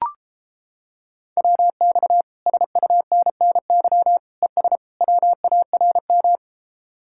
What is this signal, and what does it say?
Morse code, heard as a keyed tone, about 31 wpm, 700 Hz. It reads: WX SUNNY ES WARM